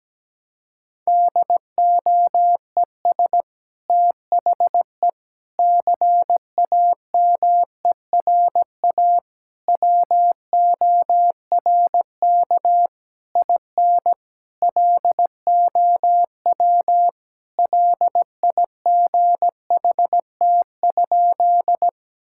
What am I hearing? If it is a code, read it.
Morse code, heard as a keyed tone, about 17 wpm, 700 Hz. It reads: DOES THE CAMERA WORK IN LOW LIGHT?
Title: Sentences from Top 1000 Words - Review Part 5 - 17wpm